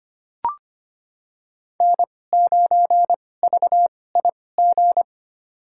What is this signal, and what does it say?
Morse code, heard as a keyed tone, about 25 wpm, spaced as for 18 wpm, 700 Hz. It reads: N9VIG
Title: USA Call Signs - 5 Characters - 2x Character Spacing - 25wpm